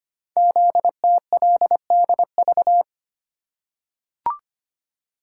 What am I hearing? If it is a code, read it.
Morse code, heard as a keyed tone, about 25 wpm, 690 Hz. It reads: ZTLDV
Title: Code Groups - 5 Letters - 25wpm